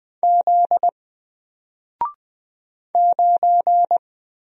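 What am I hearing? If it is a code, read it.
Morse code, heard as a keyed tone, about 20 wpm, 700 Hz. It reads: Z 9
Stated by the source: Single letter and number Morse drill at 20 wpm, from N4LCD.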